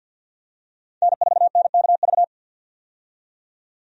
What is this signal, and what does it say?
Morse code, heard as a keyed tone, about 51 wpm, 700 Hz. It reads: N4NKV
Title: USA Call Signs - 5 Characters - 50wpm